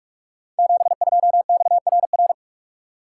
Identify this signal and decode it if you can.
Morse code, heard as a keyed tone, about 44 wpm, 700 Hz. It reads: 72XRR
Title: Code Groups - 5 Letter:Numbers - 45wpm